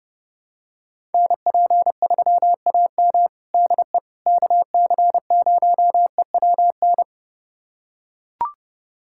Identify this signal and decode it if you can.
Morse code, heard as a keyed tone, about 32 wpm, 700 Hz. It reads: NP3AM DE KC0EWN
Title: USA Callsign DE Callsign 30wpm